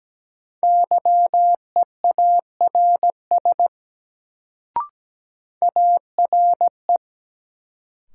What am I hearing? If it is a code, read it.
Morse code, heard as a keyed tone, about 17 wpm, 700 Hz. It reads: YEARS ARE
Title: Sets of 2 Words - New Word Drill - 17wpm